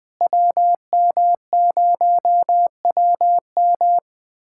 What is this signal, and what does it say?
Morse code, heard as a keyed tone, about 20 wpm, 700 Hz. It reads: WM0WM
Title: USA Callsign DE Callsign - 5 Characters - 20wpm